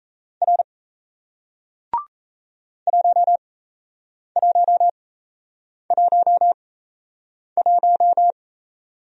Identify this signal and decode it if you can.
Morse code, heard as a keyed tone, about 37 wpm, spaced as for 8 wpm, 700 Hz. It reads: R 1111